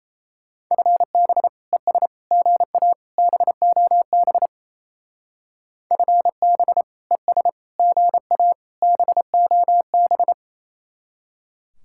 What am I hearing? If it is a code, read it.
Morse code, heard as a keyed tone, about 34 wpm, 700 Hz. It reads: FB ES GA BOB FB ES GA BOB
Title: Speed-Racing Common QSO Phrases Encore - 28wpm